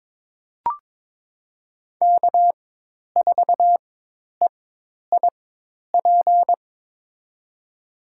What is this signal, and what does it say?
Morse code, heard as a keyed tone, about 22 wpm, spaced as for 10 wpm, 700 Hz. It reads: K4EIP
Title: USA Call Signs - 4x Character Spacing - 22wpm